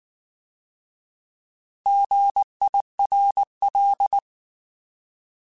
Morse code, heard as a keyed tone, about 19 wpm, 800 Hz. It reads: GIRL